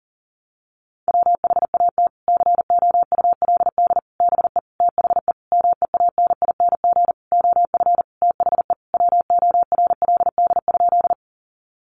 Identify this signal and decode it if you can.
Morse code, heard as a keyed tone, about 40 wpm, 700 Hz. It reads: WHAT COULD BE THE MEANING OF THE WORLD?
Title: Sentences from Top 900 Words 40wpm